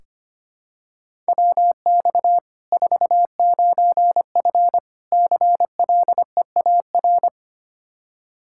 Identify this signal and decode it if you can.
Morse code, heard as a keyed tone, about 25 wpm, 700 Hz. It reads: WX 49F CLEAR
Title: Practice QSO - Exchange 2 - Info - 25wpm